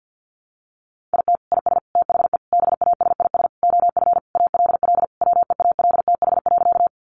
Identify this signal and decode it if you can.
Morse code, heard as a keyed tone, about 50 wpm, 700 Hz. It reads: IT IS THE BASIS OF ALL WEALTH.
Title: Sentences from Top 1200 Words 50wpm